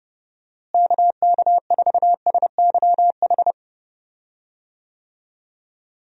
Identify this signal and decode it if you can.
Morse code, heard as a keyed tone, about 32 wpm, 700 Hz. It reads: KK4SYH
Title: USA Call Signs - 6 Characters - Repeated 3x - 30wpm